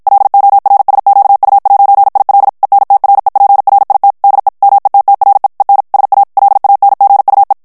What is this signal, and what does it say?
Morse code, heard as a keyed tone, about 53 wpm, 800 Hz. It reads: LORSQU'IL ENTREPRIT DE METTRE EN SA LANGUE
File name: FRANCAIS TEXTE CLAIR 2 VITESSE 50 MOTS MINUTES